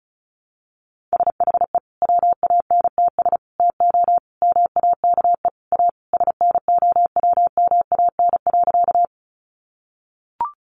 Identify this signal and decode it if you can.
Morse code, heard as a keyed tone, about 34 wpm, 700 Hz. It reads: SHE WANTS TO MAKE A SNOWMAN.
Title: Easy Phrases 35wpm